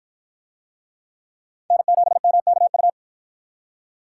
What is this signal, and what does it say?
Morse code, heard as a keyed tone, about 47 wpm, 700 Hz. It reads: N7MKU